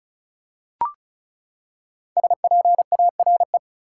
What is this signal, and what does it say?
Morse code, heard as a keyed tone, about 34 wpm, 700 Hz. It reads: SPARE
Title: Five Letter Words - 35wpm